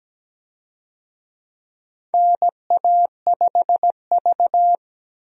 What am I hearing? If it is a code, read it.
Morse code, heard as a keyed tone, about 17 wpm, 700 Hz. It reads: NA5V
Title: USA Call Signs - 4 Characters - 17wpm